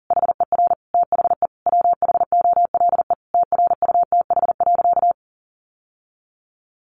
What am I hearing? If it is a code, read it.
Morse code, heard as a keyed tone, about 40 wpm, 700 Hz. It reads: HER THE WHOLE TRUTH.
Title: Sentences from Top 800 Words 40wpm